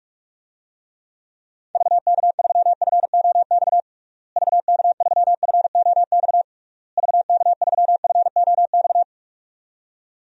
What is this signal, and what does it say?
Morse code, heard as a keyed tone, about 42 wpm, 700 Hz. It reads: VK3FOX VK3FOX VK3FOX
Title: DXCC-100 Call Signs - 6 Characters - Repeated 3x - 45wpm